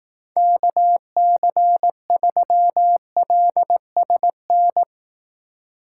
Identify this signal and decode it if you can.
Morse code, heard as a keyed tone, about 18 wpm, 700 Hz. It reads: KC3LSN
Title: Speed Racing - USA 2x3 Call Signs 15wpm